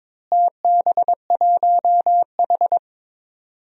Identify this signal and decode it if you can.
Morse code, heard as a keyed tone, about 22 wpm, 700 Hz. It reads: TB1H
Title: DXCC-100 Call Signs - 4 Characters - Repeated 3x - 22wpm